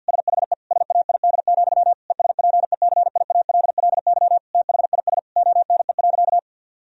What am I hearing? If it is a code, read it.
Morse code, heard as a keyed tone, about 50 wpm, 700 Hz. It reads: SHE SAID, ESPECIALLY THIS ONE.
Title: Sentences from Top 1000 Words - Review Part 4 - 50wpm